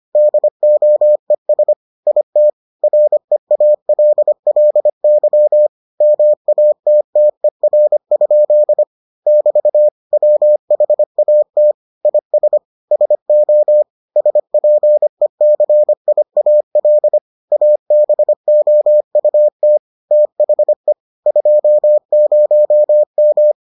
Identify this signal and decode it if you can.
Morse code, heard as a keyed tone, about 25 wpm, 600 Hz. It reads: DOES IT REALLY MATTER? <BT> WHAT IS SO SPECIAL ABOUT THE 20M